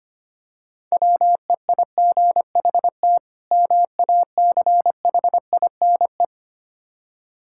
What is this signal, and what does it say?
Morse code, heard as a keyed tone, about 26 wpm, 700 Hz. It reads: WEIGHT MACHINE
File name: practice / 25wpm / Difficult Words - Speed-Racing with Hints - 25wpm